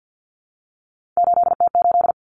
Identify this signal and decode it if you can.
Morse code, heard as a keyed tone, about 51 wpm, 700 Hz. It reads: 8T8
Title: Common Cut Numbers - 3-Digit 50wpm